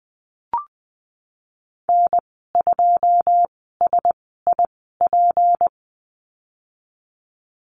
Morse code, heard as a keyed tone, about 20 wpm, spaced as for 15 wpm, 700 Hz. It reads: N2SIP
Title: 154 Lesson 25g - Letter p - Call Signs 20wpm